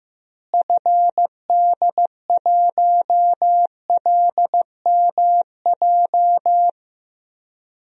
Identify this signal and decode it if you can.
Morse code, heard as a keyed tone, about 15 wpm, 700 Hz. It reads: FD1LMJ
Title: DXCC-100 Call Signs - 6 Characters - Repeated 2x - 15wpm